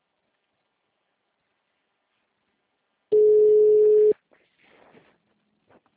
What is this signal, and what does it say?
Ringback tone (Europe, 425 Hz).
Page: Bauru - SP